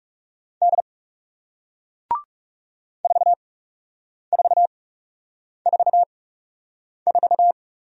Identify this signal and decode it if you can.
Morse code, heard as a keyed tone, about 38 wpm, spaced as for 8 wpm, 700 Hz. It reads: D 4444